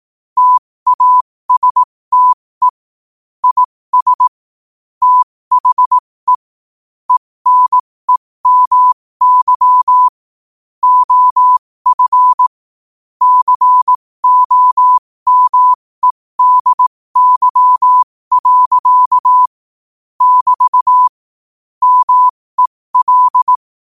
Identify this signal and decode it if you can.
Morse code, heard as a keyed tone, about 18 wpm, spaced as for 15 wpm, 1000 Hz. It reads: TASTE IS THE ENEMY OF COMEDY. <BT> MEL
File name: qotdfiles / 15wpm